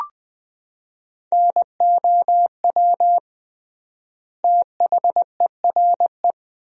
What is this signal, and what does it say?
Morse code, heard as a keyed tone, about 20 wpm, 700 Hz. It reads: NOW THERE
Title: 3x Word-Spacing - Sentences from Top 100 Words - 20wpm